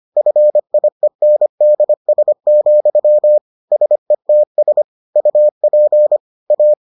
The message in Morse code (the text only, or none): FIENDS, SETS UP A